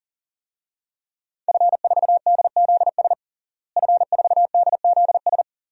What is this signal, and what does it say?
Morse code, heard as a keyed tone, about 37 wpm, 700 Hz. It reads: F4DZS F4DZS
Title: DXCC-100 Call Signs - 5 Characters - Repeated 3x - 40wpm